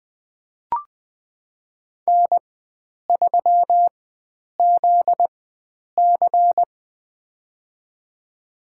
Morse code, heard as a keyed tone, about 20 wpm, spaced as for 9 wpm, 700 Hz. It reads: N3ZC
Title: USA Call Signs - 4 Characters - 4x Character Spacing - 20wpm